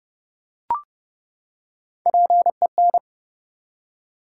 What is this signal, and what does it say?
Morse code, heard as a keyed tone, about 30 wpm, 700 Hz. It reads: PEN
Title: Three Letter Words - 30wpm